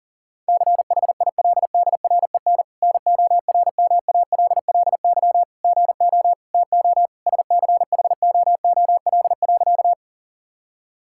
Morse code, heard as a keyed tone, about 40 wpm, 700 Hz. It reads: CHILDREN NORMALLY GO TO SCHOOL.
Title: Sentences from Top 1000 Words - Review Part 6 - 40wpm